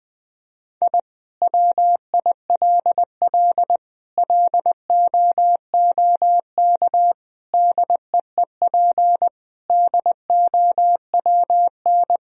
Morse code, heard as a keyed tone, about 20 wpm, 700 Hz. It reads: I WILL LOOK DEEP DOWN
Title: Sentences from Top 1000 Words - Review Part 3 - 20wpm